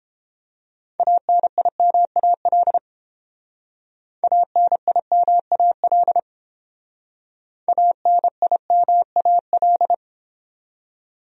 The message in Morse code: ANIMAL ANIMAL ANIMAL

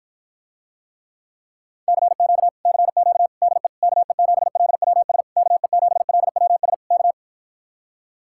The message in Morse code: CQ CQ DE KE7LWS KE7LWS K

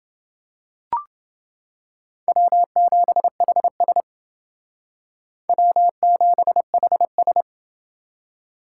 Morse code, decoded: W7HS W7HS

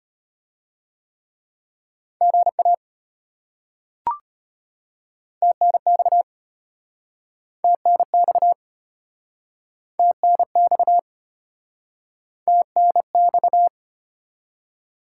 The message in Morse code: GA TNX TNX TNX TNX

